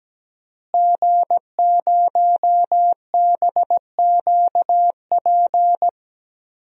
Morse code, heard as a keyed tone, about 17 wpm, 700 Hz. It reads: G0BQP
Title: DXCC-100 Call Signs - 5 Characters - Repeated 3x - 17wpm